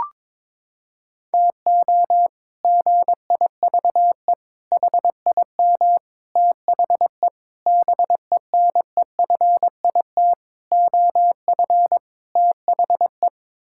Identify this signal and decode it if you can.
Morse code, heard as a keyed tone, about 22 wpm, 700 Hz. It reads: TO GIVE HIM THE BENEFIT OF THE
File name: practice / 22wpm / Sets of 8 Words - 22wpm